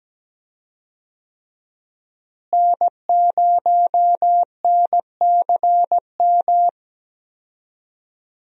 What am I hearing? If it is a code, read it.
Morse code, heard as a keyed tone, about 17 wpm, 700 Hz. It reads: N0NCM
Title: USA Call Signs - 5 Characters - Repeated 3x - 17wpm